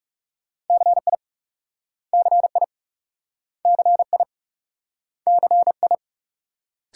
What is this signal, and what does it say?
Morse code, heard as a keyed tone, about 37 wpm, 700 Hz. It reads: CI CI CI CI
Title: Speed-Racing Multi-letter Phonemes - 30wpm